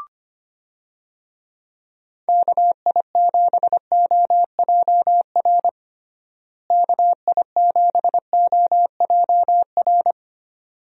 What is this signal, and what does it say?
Morse code, heard as a keyed tone, about 25 wpm, 700 Hz. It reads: KI7OJR KI7OJR
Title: USA Call Signs - 6 Characters - Repeated 2x - 25wpm